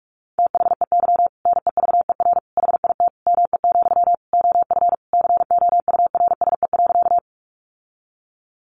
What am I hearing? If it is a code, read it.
Morse code, heard as a keyed tone, about 44 wpm, 700 Hz. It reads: THEY NEVER HIT ME, OF COURSE.
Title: Sentences from Top 1200 Words 45wpm